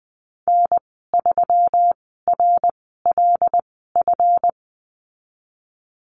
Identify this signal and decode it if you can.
Morse code, heard as a keyed tone, about 20 wpm, spaced as for 15 wpm, 700 Hz. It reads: N3RLF
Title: 147 Lesson 24g - Letter y - Call Signs 20wpm